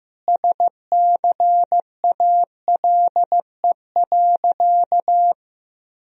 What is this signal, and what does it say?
Morse code, heard as a keyed tone, about 15 wpm, 700 Hz. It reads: SCALE.
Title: Sentences from Top 1000 Words - Review Part 4 - 15wpm